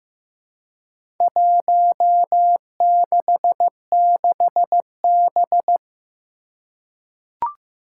Morse code, heard as a keyed tone, about 15 wpm, 700 Hz. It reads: 166B
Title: Four Challenging Letter-Number Combinations 15wpm